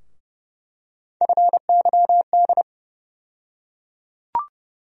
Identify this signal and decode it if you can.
Morse code, heard as a keyed tone, about 30 wpm, 700 Hz. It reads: FYD
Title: USA Call Sign Postfixes - 3 Letters 30wpm